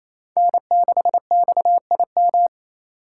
Morse code, heard as a keyed tone, about 35 wpm, 700 Hz. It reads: N6XIM
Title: Speed Racing - USA 1x3 Call Signs 28wpm